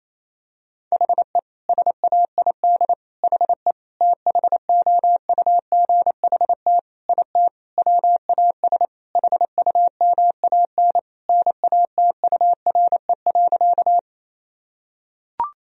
Morse code, heard as a keyed tone, about 27 wpm, 700 Hz. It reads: HE SAID HE THOUGHT IT WAS HUMAN NATURE.
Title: Sentences from Top 1000 Words - Review Part 3 - 28wpm